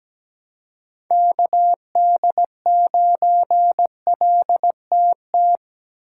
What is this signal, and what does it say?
Morse code, heard as a keyed tone, about 20 wpm, 700 Hz. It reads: KD9LTT
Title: Speed Racing - USA 2x3 Call Signs 17wpm